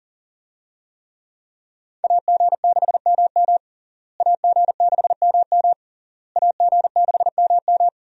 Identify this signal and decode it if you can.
Morse code, heard as a keyed tone, about 37 wpm, 700 Hz. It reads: AG6MM AG6MM AG6MM
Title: USA Call Signs - 5 Characters - Repeated 3x - 40wpm